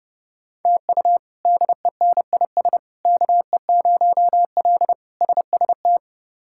Morse code, heard as a keyed tone, about 30 wpm, 700 Hz. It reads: TU DENIS KE0L SST
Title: K1USN SST QSO Practice - 30wpm